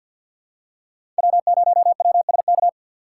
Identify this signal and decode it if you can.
Morse code, heard as a keyed tone, about 47 wpm, 690 Hz. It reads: W0WSK
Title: USA Call Signs - 5 Characters - 6x Character Spacing - 50wpm